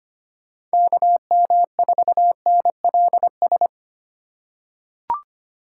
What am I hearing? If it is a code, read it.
Morse code, heard as a keyed tone, about 25 wpm, 700 Hz. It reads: KM4NLS
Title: USA Call Signs - 6 Characters - 4x Character Spacing - 25wpm